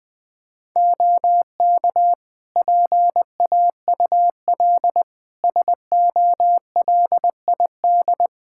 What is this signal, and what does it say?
Morse code, heard as a keyed tone, about 20 wpm, 700 Hz. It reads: OK PAUL SOLID
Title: K1USN SST QSO Practice 20wpm